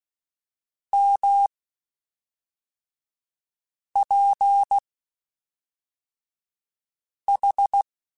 Morse code, heard as a keyed tone, about 16 wpm, spaced as for 3 wpm, 800 Hz. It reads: MPH
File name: Spaced Characters / Spaced Characters-mp3s / Weather4 14 sp @5